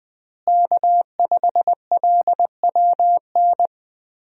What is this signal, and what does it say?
Morse code, heard as a keyed tone, about 21 wpm, 700 Hz. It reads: K5LWN
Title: USA Call Signs - 5 Characters - Repeated 3x - 20wpm